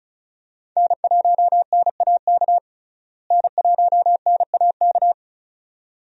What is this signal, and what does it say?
Morse code, heard as a keyed tone, about 36 wpm, 700 Hz. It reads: N1NAK N1NAK
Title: USA Call Signs - 5 Characters - Repeated 2x - 35wpm